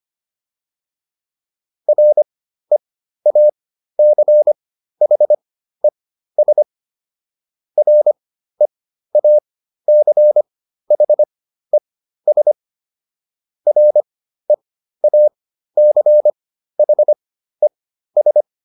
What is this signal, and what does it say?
Morse code, heard as a keyed tone, about 25 wpm, spaced as for 13 wpm, 600 Hz. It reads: REACHES REACHES REACHES